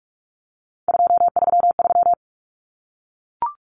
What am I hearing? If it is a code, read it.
Morse code, heard as a keyed tone, about 44 wpm, 700 Hz. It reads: 233
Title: Common Cut Numbers - 3-Digit 45wpm